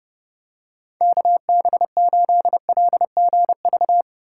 Morse code, heard as a keyed tone, about 30 wpm, 700 Hz. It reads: KB8LGV
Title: USA Call Signs - 6 Characters - 3x Character Spacing - 30wpm